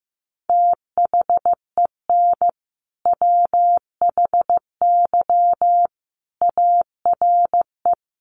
Morse code, heard as a keyed tone, about 15 wpm, 700 Hz. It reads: THEN WHY ARE
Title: Sentences from Top 200 Words 15wpm